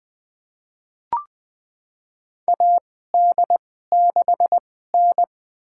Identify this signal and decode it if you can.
Morse code, heard as a keyed tone, about 20 wpm, spaced as for 14 wpm, 700 Hz. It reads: AD6N
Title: USA Call Signs - 4 Characters - 2x Character Spacing - 20wpm